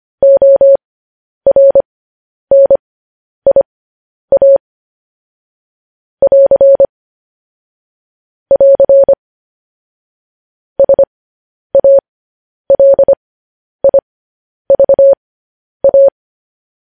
Morse code, heard as a keyed tone, about 25 wpm, spaced as for 10 wpm, 550 Hz. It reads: ORNIA <AR> <AR> SALIVA
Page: news headlines in morse code at 10 wpm